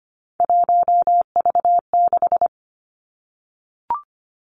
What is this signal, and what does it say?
Morse code, heard as a keyed tone, about 25 wpm, 700 Hz. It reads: 1V6
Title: Three Challenging Letter-Number Combinations 25wpm